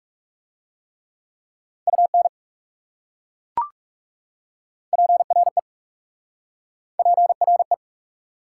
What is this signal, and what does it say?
Morse code, heard as a keyed tone, about 40 wpm, 700 Hz. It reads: UN PRE PRE